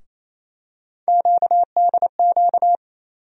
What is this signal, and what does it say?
Morse code, heard as a keyed tone, about 28 wpm, 700 Hz. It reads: QDQ
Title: USA Call Sign Postfixes - 3 Letters 28wpm